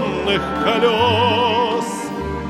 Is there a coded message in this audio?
No.